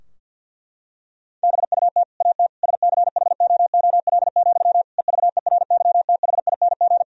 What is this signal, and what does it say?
Morse code, heard as a keyed tone, about 50 wpm, 700 Hz. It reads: BUT AT SCHOOL, EVERYTHING
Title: Sentences from Top 1000 Words - Review Part 3 - 50wpm